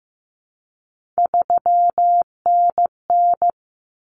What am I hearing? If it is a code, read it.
Morse code, heard as a keyed tone, about 15 wpm, 700 Hz. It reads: 3NN
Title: Common Cut Numbers - 3-Digit 15wpm